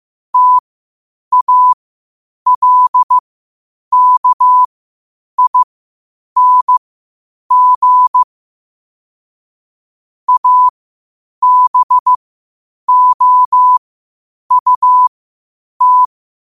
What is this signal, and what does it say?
Morse code, heard as a keyed tone, about 15 wpm, spaced as for 8 wpm, 1000 Hz. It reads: TALKING ABOUT
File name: qotdfiles / 8wpm